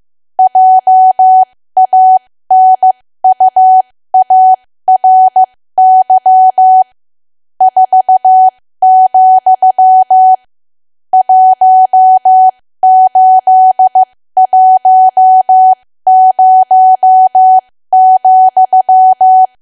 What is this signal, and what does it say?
Morse code, heard as a keyed tone, about 15 wpm, spaced as for 13 wpm, 750 Hz. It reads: JANUARY 4, 1810,
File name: cwa / W1AW-Level2 / 13WPM